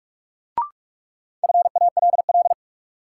Morse code, heard as a keyed tone, about 46 wpm, 700 Hz. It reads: FALL
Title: Sentences from Top 500 Words - New Word Drill - 45wpm